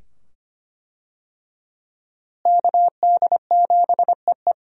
Morse code, heard as a keyed tone, about 25 wpm, 700 Hz. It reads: KD7EE
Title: USA Call Signs - 5 Characters - Repeated 2x - 25wpm